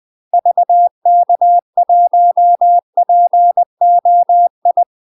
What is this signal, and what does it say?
Morse code, heard as a keyed tone, about 20 wpm, 700 Hz. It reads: VK1POI
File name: UK MASTER QSOs TRAINING / 20WPM / UK MASTER QSOs 0042